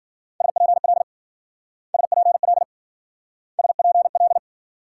Morse code, heard as a keyed tone, about 51 wpm, 700 Hz. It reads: SPL SPL SPL